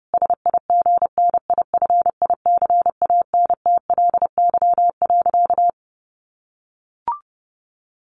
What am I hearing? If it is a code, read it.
Morse code, heard as a keyed tone, about 30 wpm, 700 Hz. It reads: SIGNIFICANTLY.
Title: Sentences from Top 1500 Words 30wpm